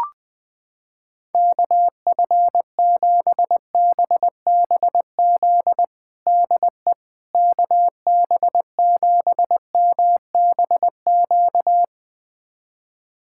Morse code, heard as a keyed tone, about 20 wpm, 700 Hz. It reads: KF7BBZ DE KB7MBQ
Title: USA Callsign DE Callsign - 6 Characters - 20wpm